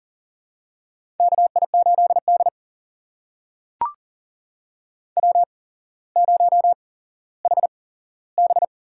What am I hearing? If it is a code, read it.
Morse code, heard as a keyed tone, about 40 wpm, spaced as for 17 wpm, 700 Hz. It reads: KI8D W0HB
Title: USA Call Signs - 4 Characters - 8x Character Spacing - 40wpm